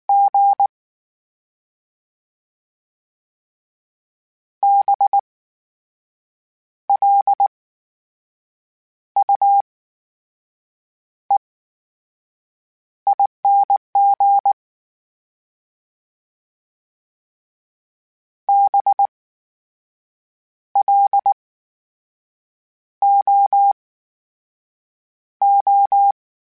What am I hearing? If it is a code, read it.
Morse code, heard as a keyed tone, about 19 wpm, spaced as for 6 wpm, 800 Hz. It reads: G BLUEING BLOO
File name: Special INGs / Special INGs-mp3s / VerySlowToFaster Ing @18